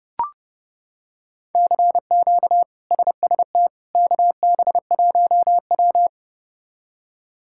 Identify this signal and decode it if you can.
Morse code, heard as a keyed tone, about 30 wpm, 700 Hz. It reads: CQ SST KB1W
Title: K1USN SST QSO Practice - 30wpm